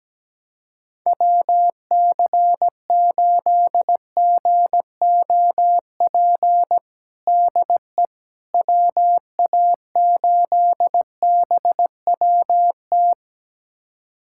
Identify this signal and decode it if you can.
Morse code, heard as a keyed tone, about 17 wpm, 700 Hz. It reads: WC8GOP DE WA8BWT
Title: USA Callsign DE Callsign 17wpm